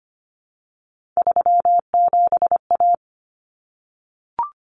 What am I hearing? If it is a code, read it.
Morse code, heard as a keyed tone, about 25 wpm, 700 Hz. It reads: 37A
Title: Common Cut Numbers - 3-Digit 25wpm